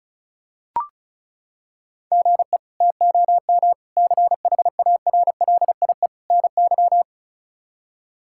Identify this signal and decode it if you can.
Morse code, heard as a keyed tone, about 36 wpm, 700 Hz. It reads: GE TOM CHARLIE NY
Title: K1USN SST QSO Practice - 35wpm